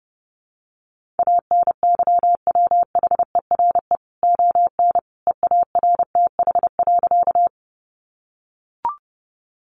Morse code, heard as a keyed tone, about 30 wpm, 700 Hz. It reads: ANYWHERE ON EARTH.
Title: Sentences from Top 1400 Words 30wpm